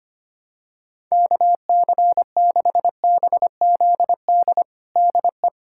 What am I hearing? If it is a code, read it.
Morse code, heard as a keyed tone, about 25 wpm, 700 Hz. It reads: KC6BZD DE
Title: USA Callsign DE Callsign - 6 Characters - 25wpm